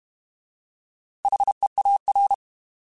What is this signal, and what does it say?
Morse code, heard as a keyed tone, about 32 wpm, 800 Hz. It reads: HEAR